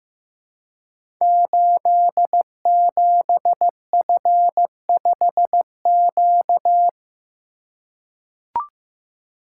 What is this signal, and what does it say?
Morse code, heard as a keyed tone, about 15 wpm, 700 Hz. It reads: 87F5Q
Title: Code Groups - 5 Letter:Numbers - 15wpm